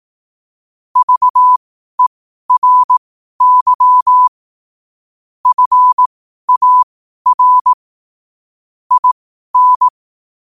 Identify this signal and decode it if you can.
Morse code, heard as a keyed tone, about 18 wpm, spaced as for 12 wpm, 1000 Hz. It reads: VERY FAR IN
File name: qotdfiles / 12wpm